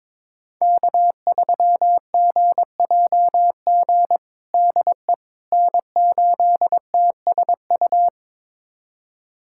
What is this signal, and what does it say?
Morse code, heard as a keyed tone, about 22 wpm, 700 Hz. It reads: K3GJG DE N8TSU
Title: USA Callsign DE Callsign - 5 Characters - 22wpm